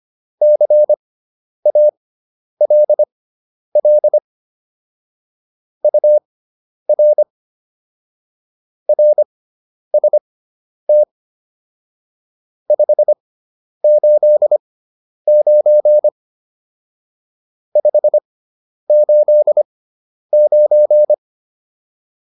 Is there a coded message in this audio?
Yes: Morse code from 0.4 to 21.1 s, heard as a keyed tone, about 25 wpm, spaced as for 10 wpm, 600 Hz. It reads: CALL UR RST 589 589